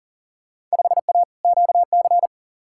Morse code, heard as a keyed tone, about 40 wpm, 700 Hz. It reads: 5A QC
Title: Field Day Exchanges Encore - 40wpm